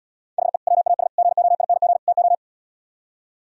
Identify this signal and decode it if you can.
Morse code, heard as a keyed tone, about 77 wpm, 700 Hz. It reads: HE WAS GOING TO